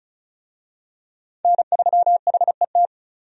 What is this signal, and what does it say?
Morse code, heard as a keyed tone, about 36 wpm, 700 Hz. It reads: N3HET
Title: USA Call Signs - 5 Characters - 35wpm